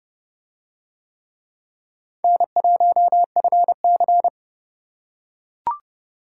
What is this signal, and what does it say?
Morse code, heard as a keyed tone, about 28 wpm, 700 Hz. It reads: N1FC